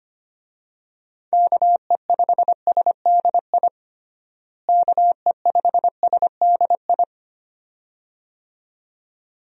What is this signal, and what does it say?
Morse code, heard as a keyed tone, about 25 wpm, 700 Hz. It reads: KE5SDI KE5SDI
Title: USA Call Signs - 6 Characters - Repeated 2x - 25wpm